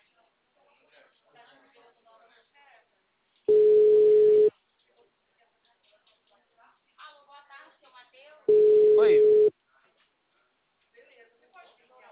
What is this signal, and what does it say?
Ringback tone (Europe, 425 Hz), with speech.
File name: out- -2007- - - .94339